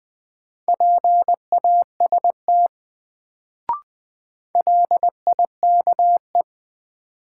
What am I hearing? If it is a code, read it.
Morse code, heard as a keyed tone, about 20 wpm, 700 Hz. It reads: PAST LIKE
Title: Sets of 2 Words - New Word Drill - 20wpm